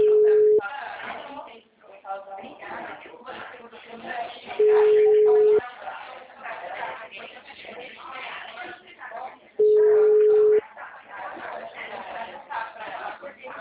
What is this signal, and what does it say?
Ringback tone (Europe, 425 Hz), with speech.